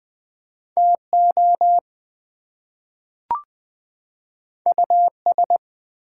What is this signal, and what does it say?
Morse code, heard as a keyed tone, about 20 wpm, 700 Hz. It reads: TO US
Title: Two Letter Words - 20wpm